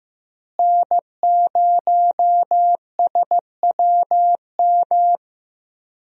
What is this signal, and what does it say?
Morse code, heard as a keyed tone, about 15 wpm, 700 Hz. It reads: N0SWM
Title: USA Call Signs - 5 Characters - Repeated 2x - 15wpm